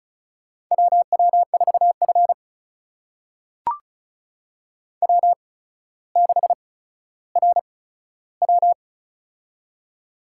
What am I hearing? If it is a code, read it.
Morse code, heard as a keyed tone, about 36 wpm, spaced as for 15 wpm, 700 Hz. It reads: WW4F W6RW
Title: USA Call Signs - 4 Characters - 8x Character Spacing - 35wpm